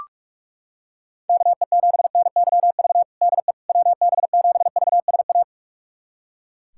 Morse code, heard as a keyed tone, about 41 wpm, 700 Hz. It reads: KE7NYV DE WB7VSA
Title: USA Callsign DE Callsign 45wpm